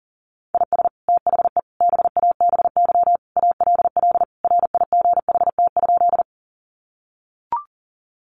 Morse code, heard as a keyed tone, about 37 wpm, 700 Hz. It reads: IS THE BABY ALL RIGHT?
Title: Sentences from Top 1300 Words 40wpm